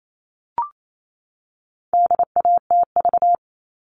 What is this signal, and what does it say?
Morse code, heard as a keyed tone, about 28 wpm, 700 Hz. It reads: DATV
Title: All Cut Numbers - 4-Digit 28wpm